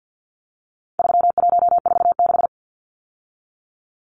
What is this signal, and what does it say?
Morse code, heard as a keyed tone, about 48 wpm, 700 Hz. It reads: 3146
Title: Numbers - 4-Digit 50wpm